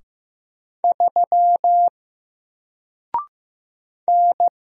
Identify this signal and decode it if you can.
Morse code, heard as a keyed tone, about 15 wpm, 700 Hz. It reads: 3 N
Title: Single Letter-Number Alt 15wpm